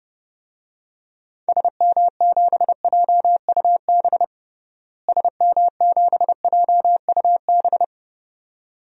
Morse code, heard as a keyed tone, about 30 wpm, 700 Hz. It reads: SM7JUB SM7JUB
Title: DXCC-100 Call Signs - 6 Characters - Repeated 2x - 30wpm